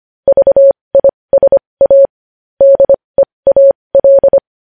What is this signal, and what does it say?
Morse code, heard as a keyed tone, about 25 wpm, spaced as for 20 wpm, 550 Hz. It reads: VISA DEAL
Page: News Headlines in Morse Code at 20 WPM